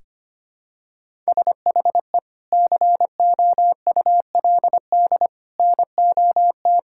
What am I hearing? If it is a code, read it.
Morse code, heard as a keyed tone, about 25 wpm, 700 Hz. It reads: SHE COULD NOT
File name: practice / 25wpm / Sentences with 2-Word Phrases - 25wpm